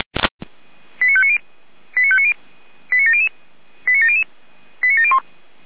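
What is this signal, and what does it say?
Five-tone selective call (ZVEI), heard as quick runs of five tones, tones of about 70 ms. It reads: 89590 89590 89700 89700 89701
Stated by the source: ZVEI five-tone call received by SDR.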